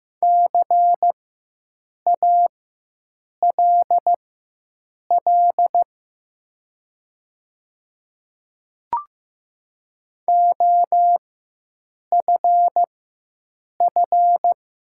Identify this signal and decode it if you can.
Morse code, heard as a keyed tone, about 15 wpm, spaced as for 7 wpm, 700 Hz. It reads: CALL OFF
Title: 4x Character Spacing - Top 200 Words - 15wpm